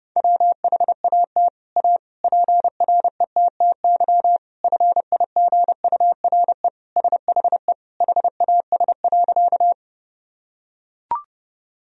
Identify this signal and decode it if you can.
Morse code, heard as a keyed tone, about 30 wpm, 700 Hz. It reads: WHAT A PRETTY FIGURE SHE HAS.
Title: Sentences from Top 1000 Words - Review Part 2 - 30wpm